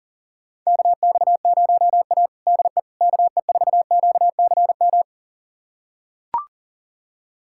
Morse code, heard as a keyed tone, about 40 wpm, 700 Hz. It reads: KX0A DE KE4QCM
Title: USA Callsign DE Callsign 40wpm